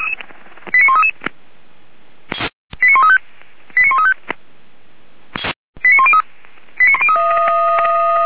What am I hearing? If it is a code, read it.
Five-tone selective call (ZVEI), heard as quick runs of five tones, tones of about 80 ms. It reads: 89133 89135 89135 89103
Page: SDR/ZVEI